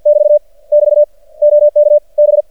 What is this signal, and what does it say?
Morse code, heard as a keyed tone, about 50 wpm, 590 Hz. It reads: <BT> <BT> OK B
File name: archived / CRSnet